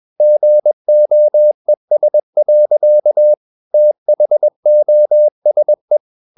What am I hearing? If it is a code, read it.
Morse code, heard as a keyed tone, about 21 wpm, 600 Hz. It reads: GOES. THOSE
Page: CW med Gnister